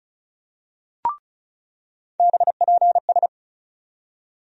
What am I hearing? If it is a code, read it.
Morse code, heard as a keyed tone, about 34 wpm, 700 Hz. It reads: BPS
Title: USA Call Sign Postfixes - 3 Letters 35wpm